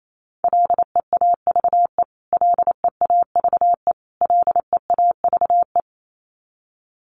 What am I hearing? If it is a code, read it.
Morse code, heard as a keyed tone, about 28 wpm, 700 Hz. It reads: LEAVE LEAVE LEAVE
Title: Top 500 Words Repeated 28wpm